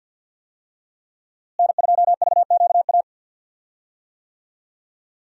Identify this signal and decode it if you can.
Morse code, heard as a keyed tone, about 50 wpm, 700 Hz. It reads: N2VQA
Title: USA Call Signs - 5 Characters - Repeated 2x - 50wpm